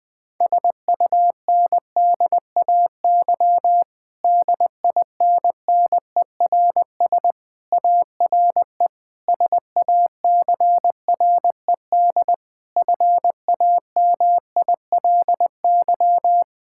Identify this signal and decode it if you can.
Morse code, heard as a keyed tone, about 20 wpm, 700 Hz. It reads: SUNDAY DINNERS ARE SACRED FAMILY
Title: Difficult Sentences with Hints - 20wpm